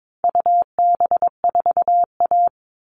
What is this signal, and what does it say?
Morse code, heard as a keyed tone, about 22 wpm, 700 Hz. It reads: UB4A
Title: All Cut Numbers - 4-Digit 22wpm